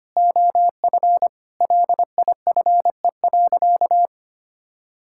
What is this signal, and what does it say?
Morse code, heard as a keyed tone, about 25 wpm, 700 Hz. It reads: OF LIFE.
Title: Sentences from Top 1000 Words - Review Part 6 - 25wpm